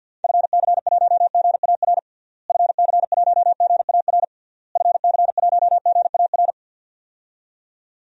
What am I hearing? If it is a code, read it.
Morse code, heard as a keyed tone, about 51 wpm, 700 Hz. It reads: FC1GAR FC1GAR FC1GAR
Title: DXCC-100 Call Signs - 6 Characters - Repeated 3x - 50wpm